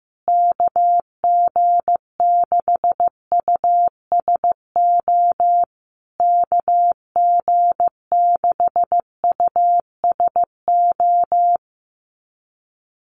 Morse code, heard as a keyed tone, about 15 wpm, 700 Hz. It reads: KG6USO KG6USO
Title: USA Call Signs Repeated 3x 15wpm